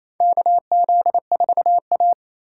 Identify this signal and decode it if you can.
Morse code, heard as a keyed tone, about 27 wpm, 700 Hz. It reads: KZ4A
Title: K1USN SST QSO Practice - 28wpm